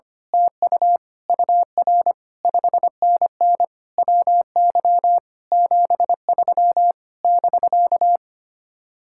Morse code, heard as a keyed tone, about 24 wpm, 700 Hz. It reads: TU UR 5NN WY 73 <BK>
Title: Speed-Racing Common QSO Phrases - 25wpm